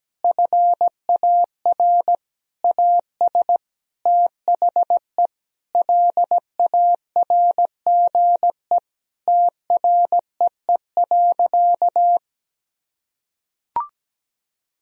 Morse code, heard as a keyed tone, about 17 wpm, 700 Hz. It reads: FAR AS THE LARGE TREE.
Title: Sentences from Top 1000 Words - Review Part 4 - 17wpm